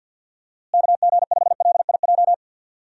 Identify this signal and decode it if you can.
Morse code, heard as a keyed tone, about 51 wpm, 700 Hz. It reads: KG5LIJ
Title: USA Call Signs - 6 Characters - Repeated 3x - 50wpm